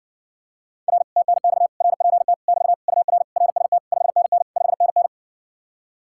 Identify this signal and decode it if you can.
Morse code, heard as a keyed tone, about 60 wpm, 700 Hz. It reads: R TNX RPT <BT> UR RST 5NN 5NN